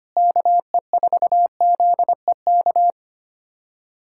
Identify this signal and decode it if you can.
Morse code, heard as a keyed tone, about 25 wpm, 700 Hz. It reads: KE4ZEK
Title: USA Callsign DE Callsign - 6 Characters - 25wpm